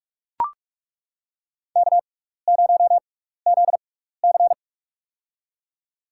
Morse code, heard as a keyed tone, about 42 wpm, spaced as for 15 wpm, 700 Hz. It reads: K0ZC
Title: USA Call Signs - 4 Characters - 6x Character Spacing - 45wpm